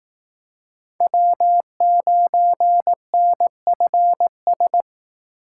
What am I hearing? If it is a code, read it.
Morse code, heard as a keyed tone, about 18 wpm, 700 Hz. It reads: W9NFS
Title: Speed Racing - USA 1x3 Call Signs 15wpm